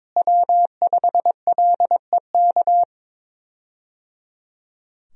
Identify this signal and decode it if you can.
Morse code, heard as a keyed tone, about 23 wpm, 700 Hz. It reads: W5LEK
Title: USA Call Signs - 5 Characters - Repeated 3x - 22wpm